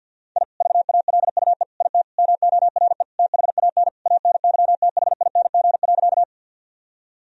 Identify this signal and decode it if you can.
Morse code, heard as a keyed tone, about 50 wpm, 700 Hz. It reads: I VALUE IT MORE THAN ANYTHING.
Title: Sentences from Top 1000 Words - Review Part 3 - 50wpm